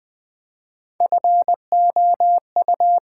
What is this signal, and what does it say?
Morse code, heard as a keyed tone, about 20 wpm, 700 Hz. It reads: FOU
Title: Build A Word - 1 Letter At A Time - 3 to 5-Letter Words - 20wpm